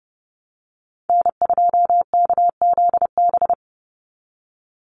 Morse code, heard as a keyed tone, about 30 wpm, 700 Hz. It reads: N2KZB
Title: USA Call Signs 30wpm